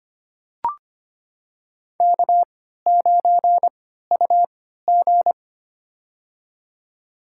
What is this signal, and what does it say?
Morse code, heard as a keyed tone, about 25 wpm, spaced as for 14 wpm, 700 Hz. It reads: K9UG
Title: USA Call Signs - 4 Characters - 3x Character Spacing - 25wpm